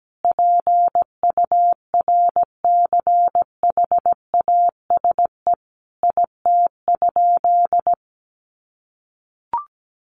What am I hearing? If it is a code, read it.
Morse code, heard as a keyed tone, about 17 wpm, 700 Hz. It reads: PURCHASE IT?
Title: Sentences from Top 800 Words 17wpm